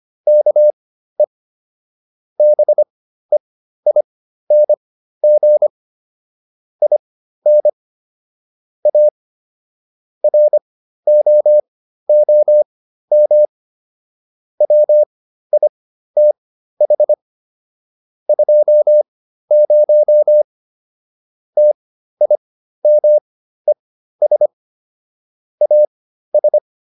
Morse code, heard as a keyed tone, about 25 wpm, spaced as for 13 wpm, 600 Hz. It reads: KE BEING IN A ROOM WITH 20 TIMES AS